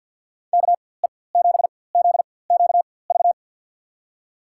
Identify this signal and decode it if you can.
Morse code, heard as a keyed tone, about 51 wpm, spaced as for 23 wpm, 700 Hz. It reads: KE7ZQV
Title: USA Call Signs - 6 Characters - 4x Character Spacing - 50wpm